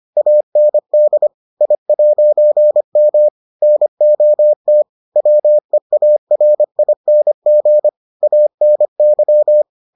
Morse code, heard as a keyed tone, about 25 wpm, 600 Hz. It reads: AND I'M NOT WEARING ANY